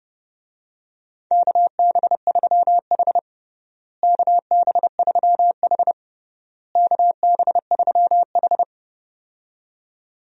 Morse code, heard as a keyed tone, about 32 wpm, 700 Hz. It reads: KB3H KB3H KB3H
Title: USA Call Signs - 4 Characters - Repeated 3x - 30wpm